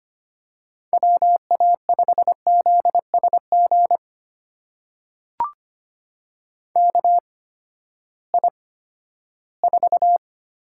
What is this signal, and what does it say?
Morse code, heard as a keyed tone, about 25 wpm, spaced as for 14 wpm, 700 Hz. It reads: WA5ZSG KI4
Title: USA Call Signs - 6 Characters - 8x Character Spacing - 25wpm